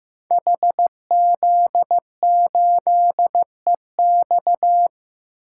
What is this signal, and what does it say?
Morse code, heard as a keyed tone, about 15 wpm, 700 Hz. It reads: HZ8EX
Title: Code Groups - 5 Letter:Numbers - 15wpm